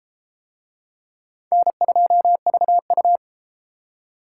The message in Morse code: N2VU

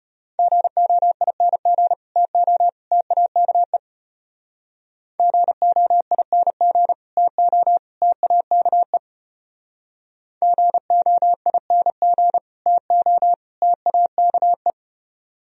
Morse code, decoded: GOING TO TAKE GOING TO TAKE GOING TO TAKE

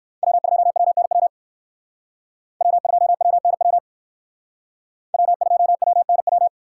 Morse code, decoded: W2WNW W2WNW W2WNW

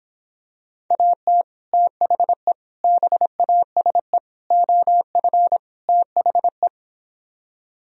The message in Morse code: AT THE BASE OF THE

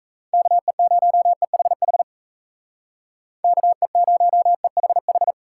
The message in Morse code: KE0EHH KE0EHH

